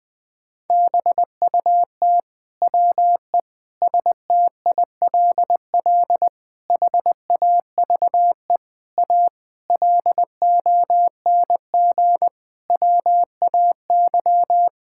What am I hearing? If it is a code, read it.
Morse code, heard as a keyed tone, about 20 wpm, 700 Hz. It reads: BUT WE STILL HAVE A LONG WAY
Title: Sets of 9 Words - 20wpm